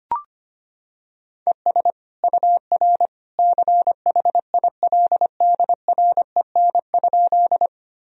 Morse code, heard as a keyed tone, about 25 wpm, 700 Hz. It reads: ES UR CHILDREN?
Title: Sentences from Top 300 Words and Abbreviations 25wpm